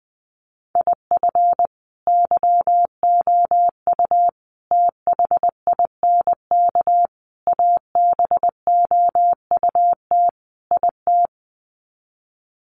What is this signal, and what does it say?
Morse code, heard as a keyed tone, about 20 wpm, 700 Hz. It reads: IF YOU THINK ABOUT IT
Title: Sets of 5 Words Encore2 20wpm